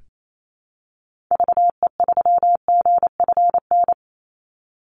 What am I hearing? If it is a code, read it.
Morse code, heard as a keyed tone, about 28 wpm, 700 Hz. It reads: VE3GFN
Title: Call Signs - 28wpm